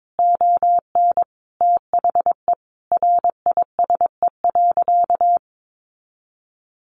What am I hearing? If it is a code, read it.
Morse code, heard as a keyed tone, about 22 wpm, 700 Hz. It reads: ON THE RISE.
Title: Sentences from Top 1300 Words 22wpm